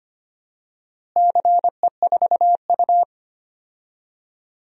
Morse code, heard as a keyed tone, about 25 wpm, 700 Hz. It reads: CE4U
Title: DXCC-100 Call Signs - 4 Characters - Repeated 2x - 25wpm